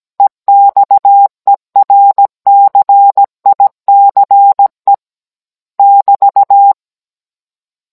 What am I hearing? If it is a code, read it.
Morse code, heard as a keyed tone, about 17 wpm, 800 Hz. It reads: EXERCICE <BT>
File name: Cours-NR19-16mm